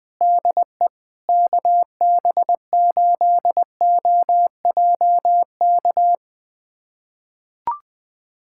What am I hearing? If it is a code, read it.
Morse code, heard as a keyed tone, about 20 wpm, 700 Hz. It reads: DE KB8OJK
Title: USA Callsign DE Callsign - 6 Characters - 20wpm